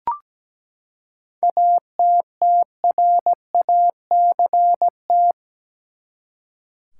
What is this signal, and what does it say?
Morse code, heard as a keyed tone, about 17 wpm, 700 Hz. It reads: ATTRACT
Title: Seven Letter Words - 17wpm